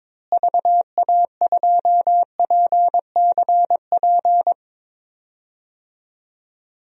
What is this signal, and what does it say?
Morse code, heard as a keyed tone, about 22 wpm, 700 Hz. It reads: VA2PCP
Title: DXCC-100 Call Signs - 6 Characters - Repeated 3x - 22wpm